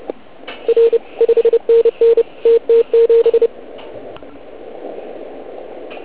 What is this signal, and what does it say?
Morse code, heard as a keyed tone, about 27 wpm, 460 Hz. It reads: R 5NN TT7